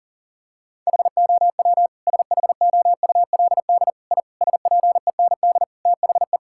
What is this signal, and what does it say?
Morse code, heard as a keyed tone, about 40 wpm, 700 Hz. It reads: HOW SHOULD I SPEND THE
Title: Sentences from Top 1000 Words - Review Part 5 - 40wpm